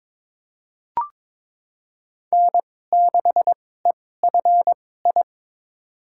Morse code, heard as a keyed tone, about 22 wpm, spaced as for 16 wpm, 700 Hz. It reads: N6EFI
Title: USA Call Signs - 5 Characters - 2x Character Spacing - 22wpm